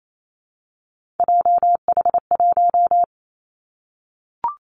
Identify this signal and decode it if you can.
Morse code, heard as a keyed tone, about 28 wpm, 700 Hz. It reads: JH1
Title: Callsign Prefixes for Top 25 DXCC - 28wpm